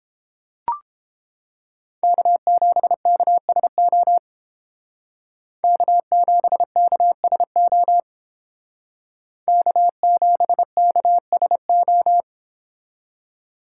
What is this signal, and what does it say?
Morse code, heard as a keyed tone, about 30 wpm, 700 Hz. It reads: K7KSO K7KSO K7KSO